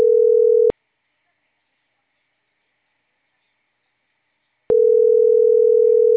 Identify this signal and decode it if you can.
Ringback tone (North America), 440 + 480 Hz.